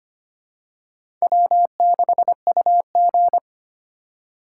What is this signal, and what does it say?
Morse code, heard as a keyed tone, about 25 wpm, 700 Hz. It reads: W6UG
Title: USA Call Signs - 4 Characters - 3x Character Spacing - 25wpm